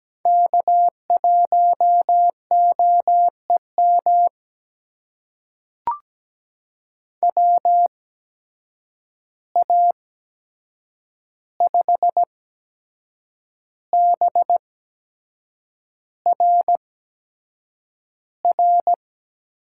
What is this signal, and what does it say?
Morse code, heard as a keyed tone, about 17 wpm, spaced as for 7 wpm, 700 Hz. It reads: K1OEM WA5BRR
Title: USA Call Signs - 8x Character Spacing - 17wpm